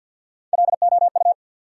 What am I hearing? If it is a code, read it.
Morse code, heard as a keyed tone, about 48 wpm, 700 Hz. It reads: LOU
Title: USA Common Names - Max 3 Letters - 50wpm